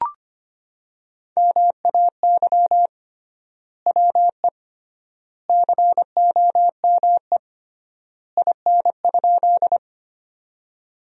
Morse code, heard as a keyed tone, about 25 wpm, 700 Hz. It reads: MAY WE COME IN?
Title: 3x Word-Spacing - Sentences from Top 100 Words - 25wpm